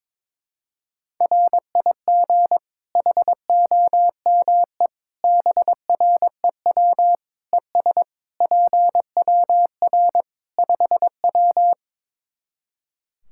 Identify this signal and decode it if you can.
Morse code, heard as a keyed tone, about 22 wpm, 700 Hz. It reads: RIG HOME BREW ES PWR 5W
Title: K1USN SST QSO Practice 22wpm